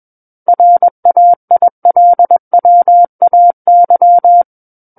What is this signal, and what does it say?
Morse code, heard as a keyed tone, about 21 wpm, 700 Hz. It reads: RAILWAY
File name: Book One / 12-Chapter 12 - 21 WPM